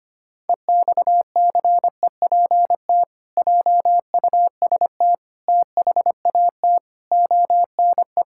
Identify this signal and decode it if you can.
Morse code, heard as a keyed tone, about 25 wpm, 700 Hz. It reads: EXCEPT JUST THAT ONE
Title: Sentences from Top 1600 Words 25wpm